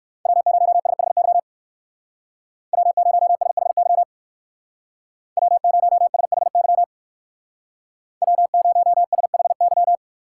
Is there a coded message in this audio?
Yes: Morse code from 0.3 to 10.0 s, heard as a keyed tone, about 58 wpm, 690 Hz. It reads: W0SHY W0SHY W0SHY W0SHY